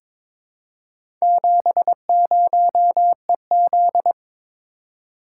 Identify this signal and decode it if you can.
Morse code, heard as a keyed tone, about 22 wpm, 700 Hz. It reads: 70EZ
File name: practice / 22wpm / Four Letter-Number Combinations - 22wpm